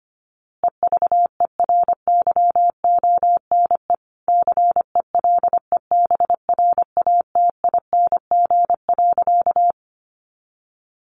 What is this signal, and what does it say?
Morse code, heard as a keyed tone, about 25 wpm, 700 Hz. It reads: EVERYONE CELEBRATING.
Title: A Sound Of Thunder 25wpm